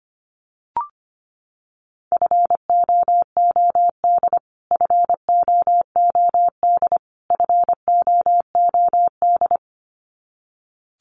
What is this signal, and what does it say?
Morse code, heard as a keyed tone, about 25 wpm, 700 Hz. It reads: FOOD FOOD FOOD
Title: Top 500 Words Repeated 25wpm